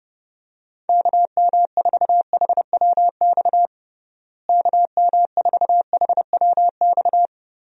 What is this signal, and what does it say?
Morse code, heard as a keyed tone, about 30 wpm, 700 Hz. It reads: KM4HWX KM4HWX
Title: USA Call Signs - 6 Characters - Repeated 2x - 30wpm